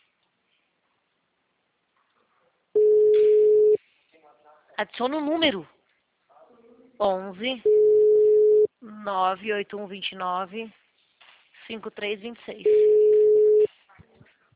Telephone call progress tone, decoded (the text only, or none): ringback tone